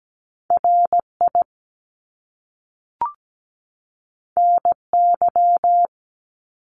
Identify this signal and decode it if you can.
Morse code, heard as a keyed tone, about 17 wpm, 700 Hz. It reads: RI NY
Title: US States 17wpm